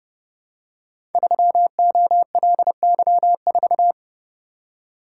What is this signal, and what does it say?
Morse code, heard as a keyed tone, about 30 wpm, 700 Hz. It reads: 3OLY4
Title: Code Groups - 5 Letter:Numbers - 30wpm